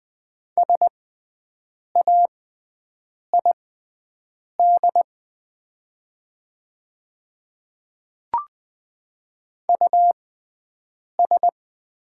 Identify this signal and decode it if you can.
Morse code, heard as a keyed tone, about 20 wpm, spaced as for 7 wpm, 700 Hz. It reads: SAID US
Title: 6x Character Spacing - Top 200 Words - 20wpm